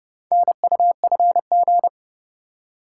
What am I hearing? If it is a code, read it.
Morse code, heard as a keyed tone, about 29 wpm, 700 Hz. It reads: NUFG